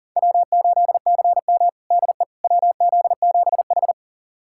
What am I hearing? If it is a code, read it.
Morse code, heard as a keyed tone, about 40 wpm, 700 Hz. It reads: W8CM DE WZ7H
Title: USA Callsign DE Callsign - 4 Characters - 40wpm